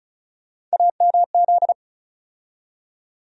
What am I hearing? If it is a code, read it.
Morse code, heard as a keyed tone, about 36 wpm, 700 Hz. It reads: AMZ
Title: USA Call Sign Postfixes - 3 Letters 35wpm